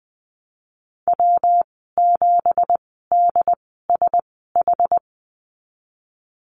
Morse code, heard as a keyed tone, about 20 wpm, spaced as for 15 wpm, 700 Hz. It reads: W7DSH
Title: 173 Lesson 28g - Number 9 - Call Signs 20wpm